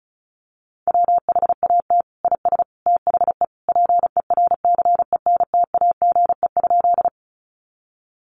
Morse code, heard as a keyed tone, about 34 wpm, 700 Hz. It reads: WHAT IS THE PERCENTAGE?
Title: Sentences from Top 1300 Words 35wpm